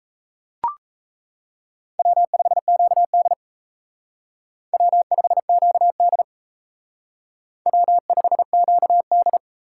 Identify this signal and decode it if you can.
Morse code, heard as a keyed tone, about 32 wpm, 700 Hz. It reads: W5QD W5QD W5QD
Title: Speed Racing - USA Call Signs 28wpm